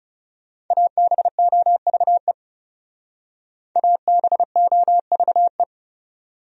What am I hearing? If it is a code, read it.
Morse code, heard as a keyed tone, about 38 wpm, 700 Hz. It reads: ABOVE ABOVE